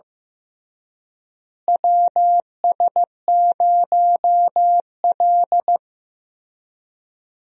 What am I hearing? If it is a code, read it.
Morse code, heard as a keyed tone, about 15 wpm, 700 Hz. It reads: WS0L
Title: USA Call Signs - 4 Characters - Repeated 3x - 15wpm